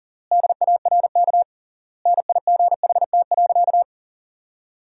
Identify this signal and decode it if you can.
Morse code, heard as a keyed tone, about 40 wpm, 700 Hz. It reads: DARK NIGHT.
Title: 3x Word-Spacing - Sentences from Top 500 Words - 40wpm